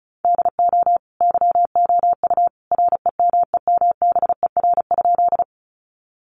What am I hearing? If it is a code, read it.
Morse code, heard as a keyed tone, about 34 wpm, 700 Hz. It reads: DO YOU REMEMBER?
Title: Sentences from Top 400 Words 35wpm